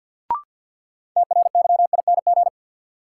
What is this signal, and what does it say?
Morse code, heard as a keyed tone, about 52 wpm, 700 Hz. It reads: TRYING
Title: Sets of 2 Words - New Word Drill - 50wpm